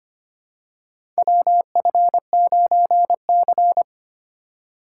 Morse code, heard as a keyed tone, about 25 wpm, 700 Hz. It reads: WF9C
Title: USA Call Signs - 4 Characters - 4x Character Spacing - 25wpm